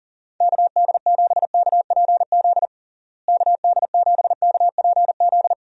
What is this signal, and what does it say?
Morse code, heard as a keyed tone, about 40 wpm, 700 Hz. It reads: KD7KPZ KD7KPZ
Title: USA Call Signs - 6 Characters - Repeated 2x - 40wpm